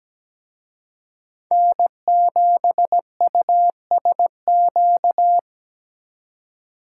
Morse code, heard as a keyed tone, about 20 wpm, 700 Hz. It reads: N7USQ
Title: Speed Racing - USA 1x3 Call Signs 17wpm